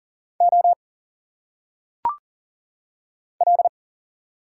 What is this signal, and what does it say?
Morse code, heard as a keyed tone, about 40 wpm, 700 Hz. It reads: O L
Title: Single Letters - Phonetic Pronunciation 40wpm